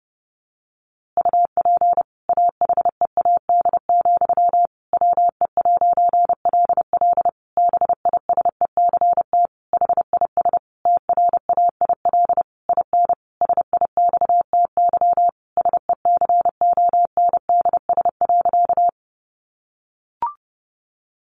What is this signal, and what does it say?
Morse code, heard as a keyed tone, about 30 wpm, 700 Hz. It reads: UP AHEAD, WE'LL BISECT HIS TRAIL IN SIXTY SECONDS.
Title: A Sound Of Thunder 30wpm